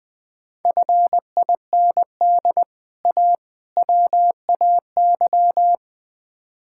Morse code, heard as a keyed tone, about 20 wpm, 700 Hz. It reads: FIND A WAY
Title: Build A Phrase - 4-Word Phrases - 20wpm